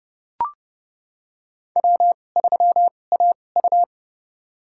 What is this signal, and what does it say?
Morse code, heard as a keyed tone, about 31 wpm, spaced as for 22 wpm, 700 Hz. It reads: W3AU
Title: USA Call Signs - 2x Character Spacing - 30wpm